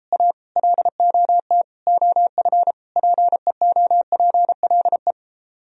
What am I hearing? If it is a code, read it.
Morse code, heard as a keyed tone, about 33 wpm, 700 Hz. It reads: A LOT OF PEOPLE